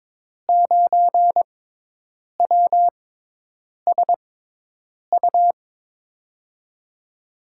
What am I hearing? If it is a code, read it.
Morse code, heard as a keyed tone, about 21 wpm, spaced as for 8 wpm, 700 Hz. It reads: 9WSU
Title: USA Call Signs - 5 Characters - 6x Character Spacing - 22wpm